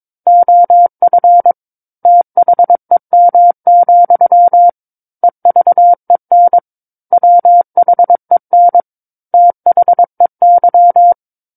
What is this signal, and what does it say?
Morse code, heard as a keyed tone, about 22 wpm, 700 Hz. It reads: OF THEM, EVEN WHEN THEY
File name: 06-Chapter-06 22 WPM